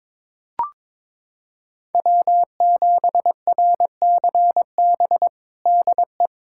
Morse code, heard as a keyed tone, about 22 wpm, 700 Hz. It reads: W7RCB DE
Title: USA Callsign DE Callsign - 5 Characters - 22wpm